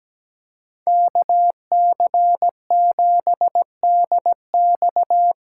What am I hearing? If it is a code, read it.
Morse code, heard as a keyed tone, about 17 wpm, 700 Hz. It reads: KC7DX
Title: USA Call Signs - 6 Characters - 8x Character Spacing - 17wpm